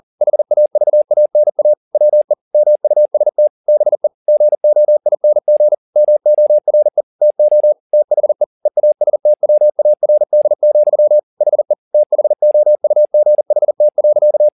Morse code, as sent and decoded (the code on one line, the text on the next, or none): .... .- ...- .- -. .- / .-- . / -- ..- ... - / -... / --. --- .. -. --. / -- --- .-. / - --- / - .... / . .- ... - .-- .- .-. -.. --..-- / .... / - .... --- ..- --. .... - .-.-.-
HAVANA WE MUST BE GOING MORE TO THE EASTWARD, HE THOUGHT.